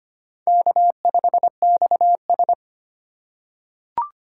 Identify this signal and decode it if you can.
Morse code, heard as a keyed tone, about 25 wpm, 700 Hz. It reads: K5XS
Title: USA Call Signs - 4 Characters - 2x Character Spacing - 25wpm